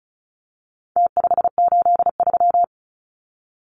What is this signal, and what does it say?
Morse code, heard as a keyed tone, about 34 wpm, 700 Hz. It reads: T583
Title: Numbers - Contest Style - 4-Digit 35wpm